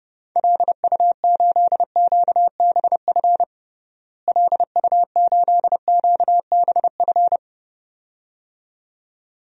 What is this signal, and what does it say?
Morse code, heard as a keyed tone, about 30 wpm, 700 Hz. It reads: LU8QBF LU8QBF
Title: DXCC-100 Call Signs - 6 Characters - Repeated 2x - 30wpm